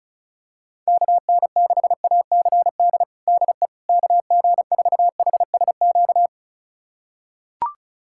Morse code, heard as a keyed tone, about 36 wpm, 700 Hz. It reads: KN6ACD DE KG4HSQ
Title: USA Callsign DE Callsign 35wpm